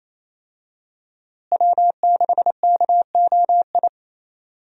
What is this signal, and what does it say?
Morse code, heard as a keyed tone, about 28 wpm, 700 Hz. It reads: W6KOI
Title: USA Call Signs - 5 Characters - Repeated 3x - 28wpm